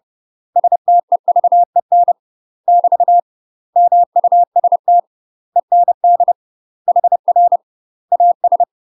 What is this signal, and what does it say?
Morse code, heard as a keyed tone, about 30 wpm, 700 Hz. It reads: STEVEN <BT> MUST END HR AS
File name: UK MASTER QSOs TRAINING/30WPM